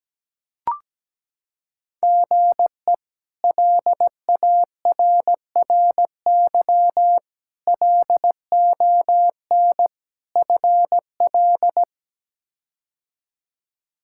Morse code, heard as a keyed tone, about 17 wpm, 700 Hz. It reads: GE LARRY LON FL
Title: K1USN SST QSO Practice - 17wpm